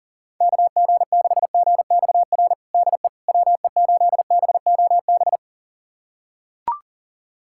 Morse code, heard as a keyed tone, about 40 wpm, 700 Hz. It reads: KG6GXR DE WE8BOB
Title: USA Callsign DE Callsign 40wpm